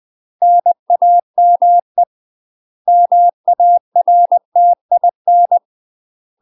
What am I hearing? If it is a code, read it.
Morse code, heard as a keyed tone, about 20 wpm, 700 Hz. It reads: NAME MARTIN
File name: UK MASTER QSOs TRAINING/20WPM